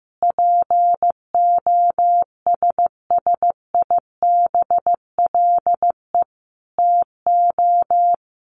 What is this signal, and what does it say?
Morse code, heard as a keyed tone, about 15 wpm, 700 Hz. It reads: POSSIBLE TO
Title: Sets of 5 Words Encore2 15wpm